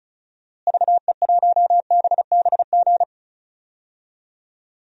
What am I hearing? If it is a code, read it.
Morse code, heard as a keyed tone, about 36 wpm, 700 Hz. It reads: VE1BBG
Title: DXCC-100 Call Signs - 6 Characters - Repeated 3x - 35wpm